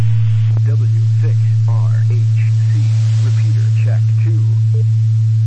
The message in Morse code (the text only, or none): none